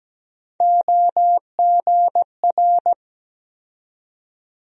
Morse code, heard as a keyed tone, about 17 wpm, 700 Hz. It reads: OGR
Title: USA Call Sign Postfixes - 3 Letters 17wpm